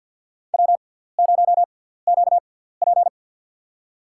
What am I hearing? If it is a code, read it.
Morse code, heard as a keyed tone, about 47 wpm, spaced as for 17 wpm, 700 Hz. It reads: W0QP
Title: USA Call Signs - 4 Characters - 6x Character Spacing - 50wpm